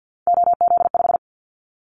Morse code, heard as a keyed tone, about 50 wpm, 700 Hz. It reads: OZ5
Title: Callsign Prefixes for Top 100 DXCC - 50wpm